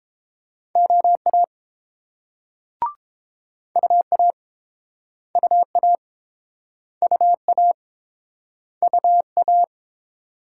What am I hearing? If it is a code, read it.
Morse code, heard as a keyed tone, about 32 wpm, 700 Hz. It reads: OA UA UA UA UA